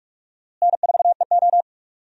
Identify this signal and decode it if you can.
Morse code, heard as a keyed tone, about 44 wpm, 700 Hz. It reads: N4EO
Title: USA Call Signs - 4 Characters - Repeated 2x - 45wpm